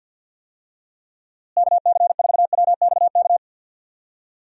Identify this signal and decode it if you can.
Morse code, heard as a keyed tone, about 50 wpm, 700 Hz. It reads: KC4WXK
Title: USA Call Signs - 6 Characters - Repeated 3x - 50wpm